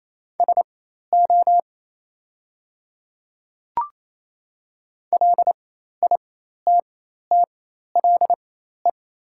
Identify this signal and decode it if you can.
Morse code, heard as a keyed tone, about 28 wpm, spaced as for 13 wpm, 700 Hz. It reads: SO LITTLE
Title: 4x Character Spacing - Top 200 Words - 28wpm